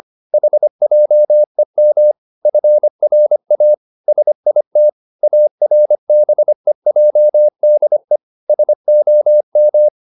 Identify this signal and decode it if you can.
Morse code, heard as a keyed tone, about 25 wpm, 600 Hz. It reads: HJEM FRA SIT ARBEJDE SOM
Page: Ferie 25 wpm